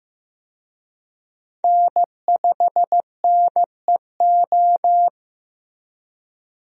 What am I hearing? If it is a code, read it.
Morse code, heard as a keyed tone, about 15 wpm, 700 Hz. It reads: N5NEO
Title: USA Call Signs - 5 Characters - Repeated 2x - 15wpm